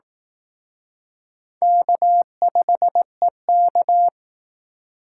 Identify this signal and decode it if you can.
Morse code, heard as a keyed tone, about 18 wpm, 700 Hz. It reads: K5EK
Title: Speed Racing - USA 1x2 Call Signs 15wpm